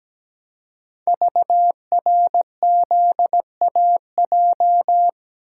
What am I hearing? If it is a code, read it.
Morse code, heard as a keyed tone, about 17 wpm, 700 Hz. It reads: VRZAJ
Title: Code Groups - 5 Letters - 17wpm